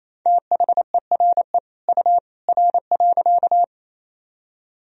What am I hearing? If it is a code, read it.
Morse code, heard as a keyed tone, about 27 wpm, 700 Hz. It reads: THERE U R.
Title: Sentences from Top 100 Words and Abbreviations 28wpm